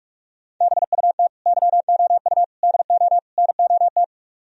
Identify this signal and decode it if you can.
Morse code, heard as a keyed tone, about 46 wpm, 700 Hz. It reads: BUT YOU DO NOT